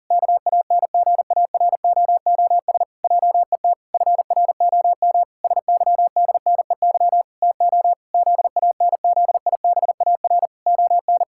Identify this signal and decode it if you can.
Morse code, heard as a keyed tone, about 37 wpm, 700 Hz. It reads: KANGAROOS JET FROM SYDNEY TO ZANZIBAR ON
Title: Pangram Challenge v2 - 40wpm